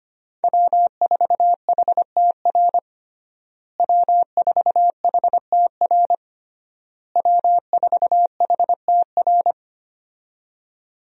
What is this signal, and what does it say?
Morse code, heard as a keyed tone, about 25 wpm, 700 Hz. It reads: W4HTR W4HTR W4HTR
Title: USA Call Signs - 5 Characters - Repeated 3x - 25wpm